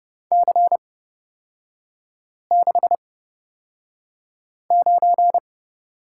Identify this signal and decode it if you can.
Morse code, heard as a keyed tone, about 30 wpm, spaced as for 5 wpm, 700 Hz. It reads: C69
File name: Lesson 37e - Letter x - Three Characters 30wpm